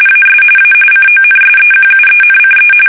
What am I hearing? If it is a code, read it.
RTTY, about 45.45 baud. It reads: AS A CONSEQUENCE
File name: fsk850 / usnavy1